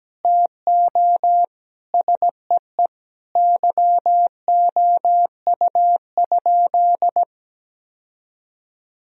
Morse code, heard as a keyed tone, about 17 wpm, 700 Hz. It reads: TO SEE YOU?
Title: Sentences from Top 1000 Words - Review Part 2 - 17wpm